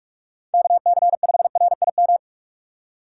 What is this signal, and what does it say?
Morse code, heard as a keyed tone, about 46 wpm, 700 Hz. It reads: KC5RIM
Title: USA Call Signs - 4x Character Spacing - 45wpm